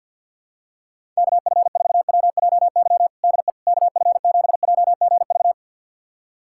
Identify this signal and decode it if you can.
Morse code, heard as a keyed tone, about 49 wpm, 700 Hz. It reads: KF4WJY DE KF7JGV
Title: USA Callsign DE Callsign 50wpm